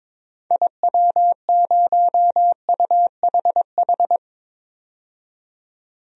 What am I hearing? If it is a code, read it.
Morse code, heard as a keyed tone, about 22 wpm, 700 Hz. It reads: IW0UHH
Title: DXCC-100 Call Signs - 6 Characters - 22wpm